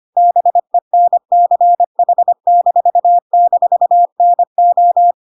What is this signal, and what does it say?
Morse code, heard as a keyed tone, about 25 wpm, 700 Hz. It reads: BENCH--NO
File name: Pinocchio / Pinocchio ch1 25 25 ik1hge